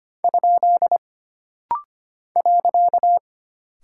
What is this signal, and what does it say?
Morse code, heard as a keyed tone, about 25 wpm, 700 Hz. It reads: ? .
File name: Single Punctuation-Prosign - Rapid-Fire - 25wpm